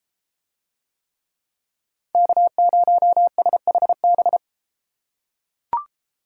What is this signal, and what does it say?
Morse code, heard as a keyed tone, about 33 wpm, 700 Hz. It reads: K0SHB